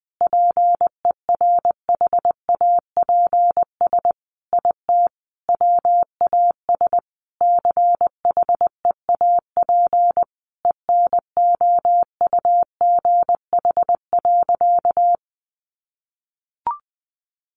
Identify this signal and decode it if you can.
Morse code, heard as a keyed tone, about 20 wpm, 700 Hz. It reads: PERHAPS IT WAS CHEAP ENOUGH.
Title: Sentences from Top 1200 Words 20wpm